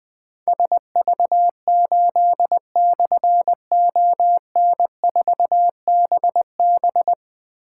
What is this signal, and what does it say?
Morse code, heard as a keyed tone, about 20 wpm, 700 Hz. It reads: SV8/ON4BB
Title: Difficult Call Signs 20wpm